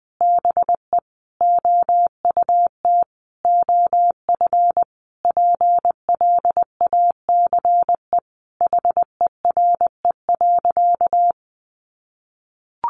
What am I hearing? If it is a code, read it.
Morse code, heard as a keyed tone, about 20 wpm, 700 Hz. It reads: BE OUT OF PLACE HERE.
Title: Sentences from Top 1500 Words 20wpm